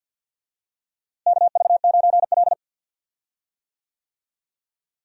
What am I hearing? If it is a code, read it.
Morse code, heard as a keyed tone, about 52 wpm, 700 Hz. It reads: KV9L
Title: USA Call Signs - 4 Characters - Repeated 2x - 50wpm